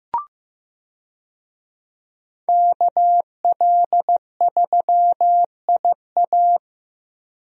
Morse code, heard as a keyed tone, about 15 wpm, 700 Hz. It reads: KL3IA
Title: USA Call Signs - 5 Characters - Repeated 3x - 15wpm